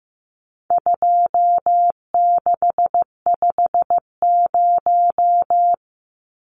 Morse code, heard as a keyed tone, about 15 wpm, 700 Hz. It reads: 2650
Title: Numbers - 4-Digit 15wpm